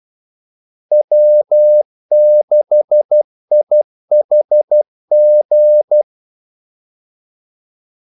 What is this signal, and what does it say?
Morse code, heard as a keyed tone, about 12 wpm, 600 Hz. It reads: W6IHG
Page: Kaldesignaler 12 wpm